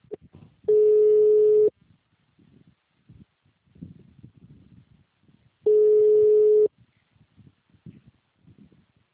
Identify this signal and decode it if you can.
Ringback tone (Europe, 425 Hz).